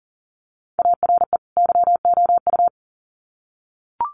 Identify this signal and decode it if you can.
Morse code, heard as a keyed tone, about 40 wpm, 700 Hz. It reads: ARE YOU
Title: Sets of 2 Words 40wpm